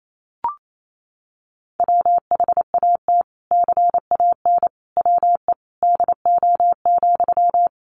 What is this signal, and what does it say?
Morse code, heard as a keyed tone, about 28 wpm, 700 Hz. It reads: WHAT CAN WE DO,
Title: Sentences from Top 100 Words 28wpm